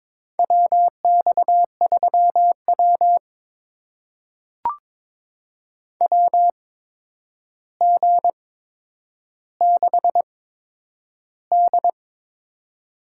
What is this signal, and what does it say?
Morse code, heard as a keyed tone, about 22 wpm, spaced as for 9 wpm, 700 Hz. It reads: WX3W WG6D
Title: USA Call Signs - 4 Characters - 8x Character Spacing - 22wpm